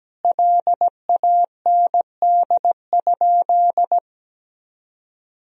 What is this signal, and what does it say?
Morse code, heard as a keyed tone, about 17 wpm, 700 Hz. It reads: LAND?
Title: Sentences from Top 1000 Words - Review Part 3 - 17wpm